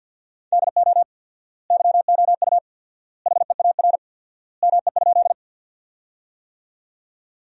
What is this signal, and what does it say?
Morse code, heard as a keyed tone, about 50 wpm, 700 Hz. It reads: DO YOU HEAR ME?
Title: 4x Word-Spacing - Sentences from Top 300 Words - 50wpm